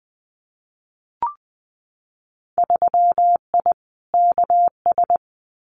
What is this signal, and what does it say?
Morse code, heard as a keyed tone, about 20 wpm, 700 Hz. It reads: 3I KS
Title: Winter Field Day Exchanges Encore 20wpm